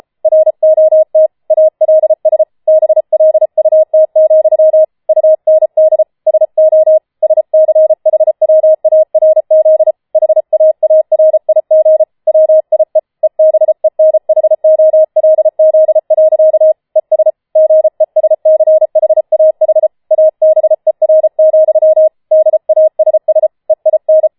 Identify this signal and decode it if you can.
Morse code, heard as a keyed tone, about 33 wpm, 630 Hz. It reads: ROT ALS BLUT, UND SO SCHWARZ HAARIG WIE EBENHOLZ. ES GESCHAH ABER, DASS EIN